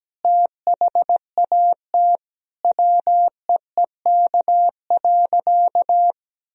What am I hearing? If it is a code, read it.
Morse code, heard as a keyed tone, about 17 wpm, 700 Hz. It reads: THAT WEEK.
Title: Sentences from Top 1000 Words - Review Part 2 - 17wpm